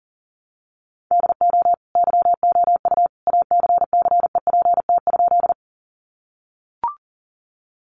Morse code, heard as a keyed tone, about 40 wpm, 700 Hz. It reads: DO YOU ACCEPT?
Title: Sentences from Top 1300 Words 40wpm